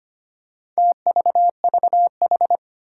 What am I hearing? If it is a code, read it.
Morse code, heard as a keyed tone, about 25 wpm, 700 Hz. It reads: TVVH
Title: Four Challenging Letter-Number Combinations 25wpm